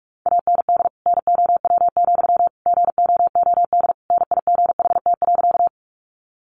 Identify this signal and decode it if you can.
Morse code, heard as a keyed tone, about 44 wpm, 700 Hz. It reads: AND NOW, GOOD NIGHT.
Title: Sentences from Top 300 Words 45wpm